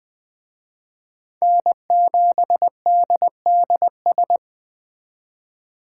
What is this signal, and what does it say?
Morse code, heard as a keyed tone, about 21 wpm, 700 Hz. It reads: N7DDS